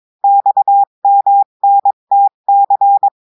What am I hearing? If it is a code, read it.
Morse code, heard as a keyed tone, about 22 wpm, 800 Hz. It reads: XMNTC
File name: cours-du-mercredi / audios / cours-853-du-2-mai-2024-etxte chiffre et texte clair-a-20-wpm